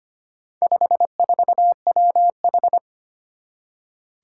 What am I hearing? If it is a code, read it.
Morse code, heard as a keyed tone, about 25 wpm, 700 Hz. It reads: 54WH
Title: Four Challenging Letter-Number Combinations 25wpm